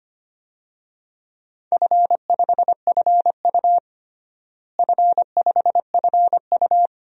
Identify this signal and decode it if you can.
Morse code, heard as a keyed tone, about 25 wpm, 700 Hz. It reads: F5FU F5FU
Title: DXCC-100 Call Signs - 4 Characters - Repeated 2x - 25wpm